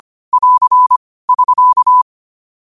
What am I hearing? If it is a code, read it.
Morse code, heard as a keyed tone, about 25 wpm, 1000 Hz. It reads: <AR> <SK>
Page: New quotes every day in morse code at 25 Words per minute.